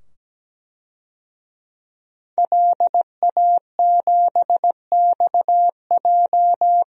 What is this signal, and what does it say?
Morse code, heard as a keyed tone, about 17 wpm, 700 Hz. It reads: LA7XJ
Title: DXCC-100 Call Signs - 5 Characters - 17wpm